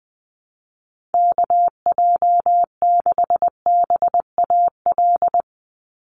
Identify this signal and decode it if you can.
Morse code, heard as a keyed tone, about 20 wpm, 700 Hz. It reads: KJ6BAL
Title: USA Call Signs 20wpm